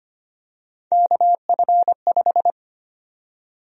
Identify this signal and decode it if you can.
Morse code, heard as a keyed tone, about 25 wpm, 700 Hz. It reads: KF5
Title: USA Call Sign Prefixes - 25wpm